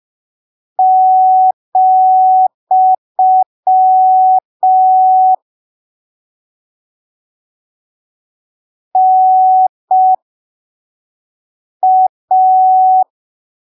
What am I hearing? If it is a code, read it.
Morse code, heard as a keyed tone, about 5 wpm, spaced as for 3 wpm, 750 Hz. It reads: , NA
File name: arquivos / texto / texto 08 05ppm